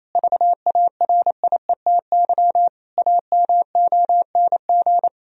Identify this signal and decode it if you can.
Morse code, heard as a keyed tone, about 28 wpm, 700 Hz. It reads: VARIETY AMONG
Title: Sentences from Top 1000 Words - Review Part 4 - 28wpm